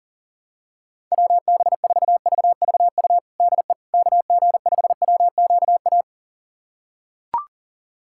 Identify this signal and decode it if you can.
Morse code, heard as a keyed tone, about 40 wpm, 700 Hz. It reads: WB4VVU DE KG5WQA
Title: USA Callsign DE Callsign - 6 Characters - 40wpm